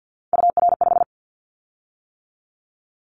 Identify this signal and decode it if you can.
Morse code, heard as a keyed tone, about 50 wpm, 700 Hz. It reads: UR5
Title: Callsign Prefixes for Top 25 DXCC - 50wpm